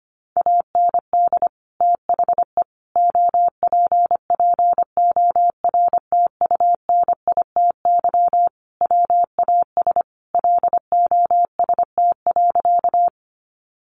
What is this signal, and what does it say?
Morse code, heard as a keyed tone, about 25 wpm, 700 Hz. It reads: AND THE OPPORTUNITY WAS LOST.
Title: Sentences from Top 1400 Words 25wpm